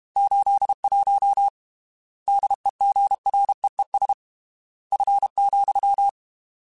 Morse code, heard as a keyed tone, about 32 wpm, 800 Hz. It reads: 81 DEGREES F,
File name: Weather Sets / Weather Sets-mp3s / Weather 5 12 @30